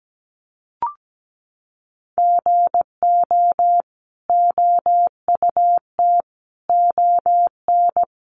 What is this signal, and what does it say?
Morse code, heard as a keyed tone, about 17 wpm, 700 Hz. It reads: GO OUT ON
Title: A Sound Of Thunder 17wpm